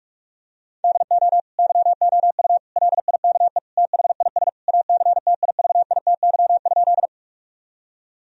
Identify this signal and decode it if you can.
Morse code, heard as a keyed tone, about 46 wpm, 700 Hz. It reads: DO YOU LIKE THIS ACTIVITY?
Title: Sentences from Top 1000 Words - Review Part 4 - 45wpm